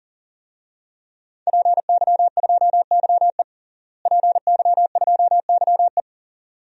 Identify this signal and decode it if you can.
Morse code, heard as a keyed tone, about 37 wpm, 700 Hz. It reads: PY2YE PY2YE
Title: DXCC-100 Call Signs - 5 Characters - Repeated 2x - 40wpm